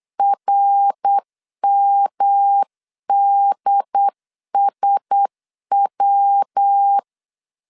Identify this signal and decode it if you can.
Morse code, heard as a keyed tone, about 8 wpm, 800 Hz. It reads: RMDSW
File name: CW / DET / 1 6 1500C S C 8wpm 5219